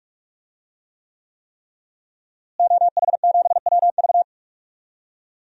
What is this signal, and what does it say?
Morse code, heard as a keyed tone, about 48 wpm, 700 Hz. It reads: OH7WV
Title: DXCC-100 Call Signs - 5 Characters - Repeated 3x - 45wpm